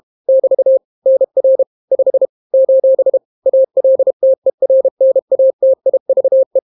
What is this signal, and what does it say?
Morse code, heard as a keyed tone, about 32 wpm, 530 Hz. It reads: <BT> NR 5 : ALTERNATIVE